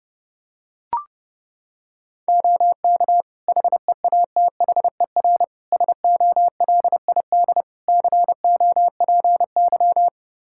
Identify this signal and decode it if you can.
Morse code, heard as a keyed tone, about 32 wpm, 700 Hz. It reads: OK HEATHER SOLID COPY
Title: K1USN SST QSO Practice 30wpm